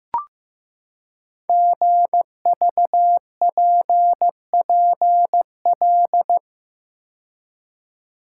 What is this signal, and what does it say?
Morse code, heard as a keyed tone, about 15 wpm, 700 Hz. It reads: GVPPL
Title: Code Groups - 5 Letters - 15wpm